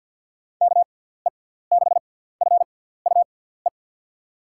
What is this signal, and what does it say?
Morse code, heard as a keyed tone, about 54 wpm, spaced as for 17 wpm, 700 Hz. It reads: KE6FUE
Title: USA Call Signs - 6 Characters - 6x Character Spacing - 50wpm